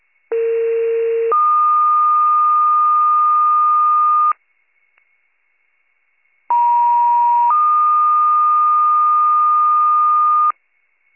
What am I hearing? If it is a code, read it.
Two-tone page, heard as a one-second tone, then a three-second tone, about 460 then 1220 Hz, about 950 then 1220 Hz. It is heard from 0.3 to 10.5 s.